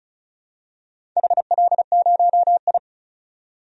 Morse code, heard as a keyed tone, about 35 wpm, 700 Hz. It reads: HL0I
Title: DXCC-100 Call Signs - 4 Characters - 35wpm